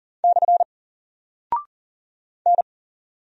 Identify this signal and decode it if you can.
Morse code, heard as a keyed tone, about 39 wpm, 700 Hz. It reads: / N
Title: Single Character (Letter:Number:Punctuation:Prosign) 40wpm